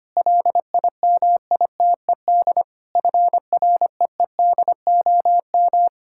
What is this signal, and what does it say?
Morse code, heard as a keyed tone, about 25 wpm, 700 Hz. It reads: LIMITED FREEDOM
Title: Sentences from Top 1000 Words - Review Part 6 - 25wpm